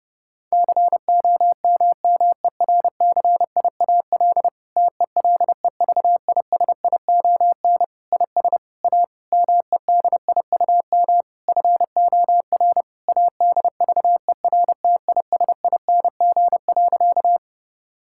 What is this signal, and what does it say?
Morse code, heard as a keyed tone, about 30 wpm, 700 Hz. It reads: COMMERCIAL TELEVISION IS A MEDIUM FOR ADVERTISING.
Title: Sentences from Top 1000 Words - Review Part 6 - 30wpm